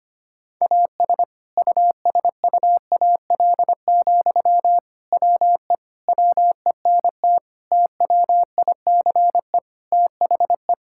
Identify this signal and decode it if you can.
Morse code, heard as a keyed tone, about 25 wpm, 700 Hz. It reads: AS USUAL, WE WENT TWICE THE
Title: Sentences from Top 1600 Words 25wpm